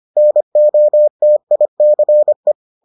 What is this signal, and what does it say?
Morse code, heard as a keyed tone, about 25 wpm, 600 Hz. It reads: NOTICE